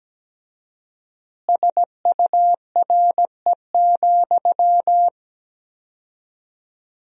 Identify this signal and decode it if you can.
Morse code, heard as a keyed tone, about 17 wpm, 700 Hz. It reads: SURE,
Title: 4x Word-Spacing - Sentences from Top 500 Words - 17wpm